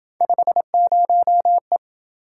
Morse code, heard as a keyed tone, about 27 wpm, 700 Hz. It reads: 50E